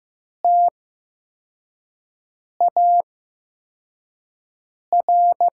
Morse code, heard as a keyed tone, about 15 wpm, spaced as for 4 wpm, 700 Hz. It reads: TAR
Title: 8x Character Spacing - Top 500 Words - 15wpm